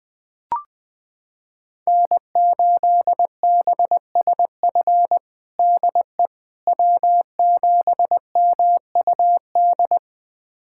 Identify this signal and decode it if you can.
Morse code, heard as a keyed tone, about 21 wpm, 700 Hz. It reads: N8BSF DE W7MUD
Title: USA Callsign DE Callsign - 5 Characters - 20wpm